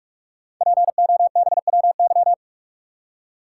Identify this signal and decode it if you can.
Morse code, heard as a keyed tone, about 44 wpm, 690 Hz. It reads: POBWY
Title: Code Groups - 5 Letters - 45wpm